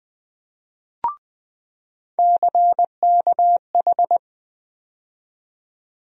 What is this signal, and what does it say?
Morse code, heard as a keyed tone, about 21 wpm, 700 Hz. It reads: CKH
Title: USA Call Sign Postfixes - 3 Letters 20wpm